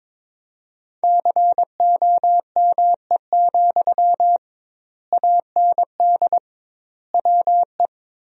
Morse code, heard as a keyed tone, about 22 wpm, 700 Hz. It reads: COME, AND WE
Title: 2x Word-Spacing - Sentences from Top 500 Words - 22wpm